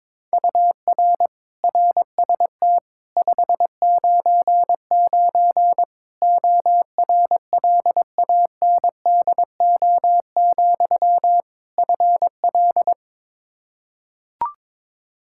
Morse code, heard as a keyed tone, about 22 wpm, 700 Hz. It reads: UR RST 599 ORLANDO, FL
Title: Practice QSO - Exchange 1 - Beginning - 22wpm